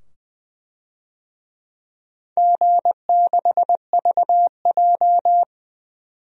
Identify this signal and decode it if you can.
Morse code, heard as a keyed tone, about 20 wpm, 700 Hz. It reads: G6VJ
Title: DXCC-100 Call Signs - 4 Characters - 20wpm